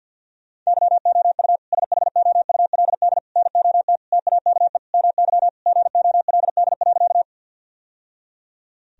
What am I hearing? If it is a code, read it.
Morse code, heard as a keyed tone, about 50 wpm, 700 Hz. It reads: YOU SHOULD NOT TAKE MY GOLD.
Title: Sentences from Top 1000 Words - Review Part 3 - 50wpm